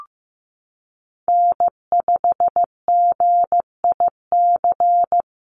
Morse code, heard as a keyed tone, about 14 wpm, 700 Hz. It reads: N5GIC
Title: USA Call Signs Repeated 2x 15wpm